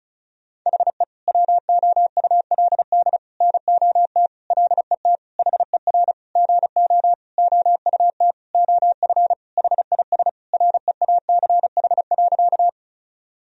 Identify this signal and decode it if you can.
Morse code, heard as a keyed tone, about 34 wpm, 700 Hz. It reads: HE WOULD NOT LET HER GO OUT OF HIS REACH.
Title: Sentences from Top 1000 Words - Review Part 2 - 35wpm